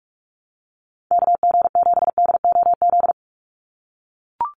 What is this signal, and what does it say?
Morse code, heard as a keyed tone, about 44 wpm, 700 Hz. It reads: KG7DOZ
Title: USA Call Signs 45wpm